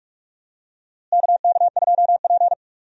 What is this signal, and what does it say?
Morse code, heard as a keyed tone, about 46 wpm, 700 Hz. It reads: KK2P
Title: USA Call Signs - 4 Characters - Repeated 3x - 45wpm